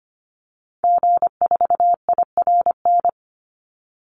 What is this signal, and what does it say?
Morse code, heard as a keyed tone, about 25 wpm, 700 Hz. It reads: G4IRN
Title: Call Signs - 25wpm